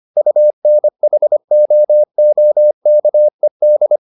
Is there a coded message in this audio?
Yes: Morse code from 0.2 to 4.0 s, heard as a keyed tone, about 25 wpm, 600 Hz. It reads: UNHOOKED